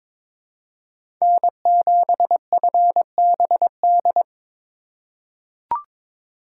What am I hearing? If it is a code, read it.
Morse code, heard as a keyed tone, about 22 wpm, 700 Hz. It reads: N7FBD
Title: USA Call Signs - 5 Characters - 3x Character Spacing - 22wpm